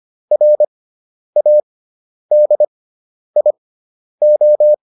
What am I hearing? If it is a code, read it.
Morse code, heard as a keyed tone, about 25 wpm, spaced as for 10 wpm, 600 Hz. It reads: RADIO
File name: SS213 10